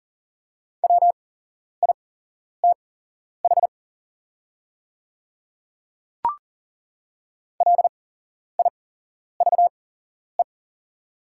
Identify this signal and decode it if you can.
Morse code, heard as a keyed tone, about 40 wpm, spaced as for 11 wpm, 700 Hz. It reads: WITH LIVE
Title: 8x Character Spacing - Top 200 Words - 40wpm